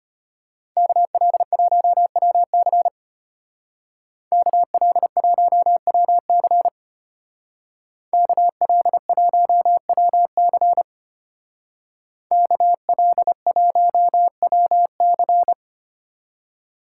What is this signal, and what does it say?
Morse code, heard as a keyed tone, about 31 wpm, 700 Hz. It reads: KL1WC KL1WC KL1WC KL1WC
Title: Speed Racing - USA 2x2 Call Signs 25wpm